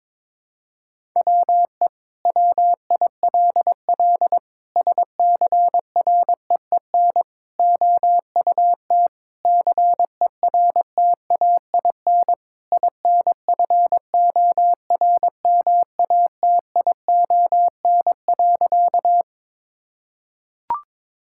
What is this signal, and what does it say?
Morse code, heard as a keyed tone, about 22 wpm, 700 Hz. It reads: WE WILL SCREEN OUT CERTAIN INFORMATION.
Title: Sentences from Top 1000 Words - Review Part 6 - 22wpm